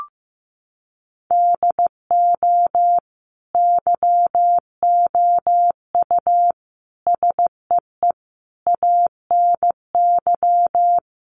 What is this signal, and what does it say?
Morse code, heard as a keyed tone, about 15 wpm, 700 Hz. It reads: DO YOU SEE ANY
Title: Sentences from Top 500 Words 15wpm